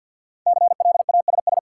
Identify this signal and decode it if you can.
Morse code, heard as a keyed tone, about 52 wpm, 700 Hz. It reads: CLASS
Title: Sentences from Top 1400 Words - New Word Drill Alt - 50wpm